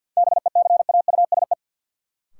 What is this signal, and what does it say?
Morse code, heard as a keyed tone, about 53 wpm, 700 Hz. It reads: BECAUSE
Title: Sets of 2 Words - New Word Drill - 50wpm